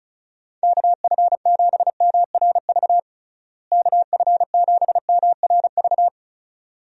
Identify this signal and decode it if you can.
Morse code, heard as a keyed tone, about 36 wpm, 700 Hz. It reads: KF7MRV KF7MRV
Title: USA Call Signs - 6 Characters - Repeated 2x - 35wpm